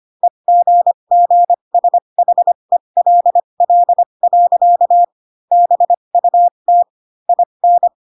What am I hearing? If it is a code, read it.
Morse code, heard as a keyed tone, about 25 wpm, spaced as for 22 wpm, 700 Hz. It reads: EGGSHELL. BUT IN